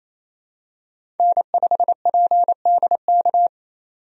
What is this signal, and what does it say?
Morse code, heard as a keyed tone, about 28 wpm, 700 Hz. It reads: N5PDK
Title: USA Call Signs - 5 Characters - 28wpm